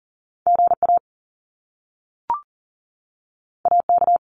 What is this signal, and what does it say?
Morse code, heard as a keyed tone, about 38 wpm, 700 Hz. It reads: GA AK
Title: US States 40wpm